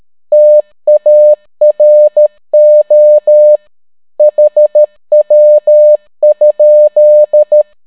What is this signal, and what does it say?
Morse code, heard as a keyed tone, about 13 wpm, 600 Hz. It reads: TARO HW?